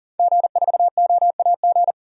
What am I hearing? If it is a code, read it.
Morse code, heard as a keyed tone, about 37 wpm, 700 Hz. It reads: G4OAG
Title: DXCC-100 Call Signs - 5 Characters - 40wpm